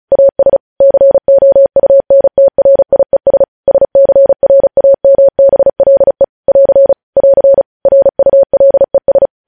Morse code, heard as a keyed tone, about 36 wpm, 550 Hz. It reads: AS COUNTRIES SCRAMBLE <AR> <AR> RULES
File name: podcast / News headlines-1x-35x35